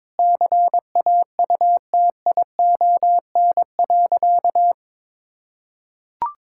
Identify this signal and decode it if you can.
Morse code, heard as a keyed tone, about 22 wpm, 700 Hz. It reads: CAUTION.
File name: Sentences with 7-Word Phrases - 22wpm